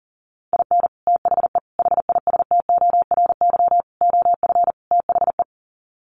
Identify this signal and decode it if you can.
Morse code, heard as a keyed tone, about 40 wpm, 700 Hz. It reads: IN THE HISTORY OF THE
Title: Sets of 5 Words 40wpm